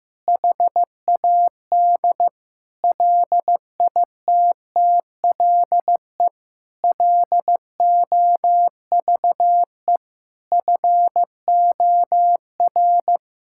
Morse code, heard as a keyed tone, about 16 wpm, 700 Hz. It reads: HAD LITTLE LOVE FOR
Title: Sentences from Top 1600 Words 15wpm